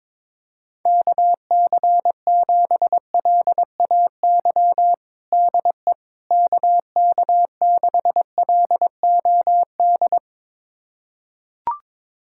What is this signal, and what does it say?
Morse code, heard as a keyed tone, about 21 wpm, 700 Hz. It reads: KC7LAY DE KK6LOD
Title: USA Callsign DE Callsign - 6 Characters - 22wpm